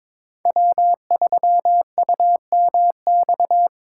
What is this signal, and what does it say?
Morse code, heard as a keyed tone, about 22 wpm, 700 Hz. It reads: W3UMX
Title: USA Call Signs - 5 Characters - 2x Character Spacing - 22wpm